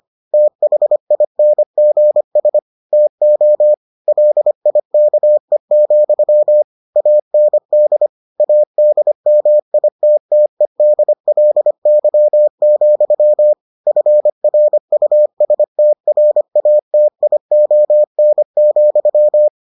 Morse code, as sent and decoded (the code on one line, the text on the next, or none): - .... .. -. --. ... / - --- / .-.. .. -.- . --..-- / .- -. -.. / .- -.. -- .. - - . -.. .-.. -.-- --..-- / ..-. .-. ..- ... - .-. .- - .. --- -. --..--
THINGS TO LIKE, AND ADMITTEDLY, FRUSTRATION,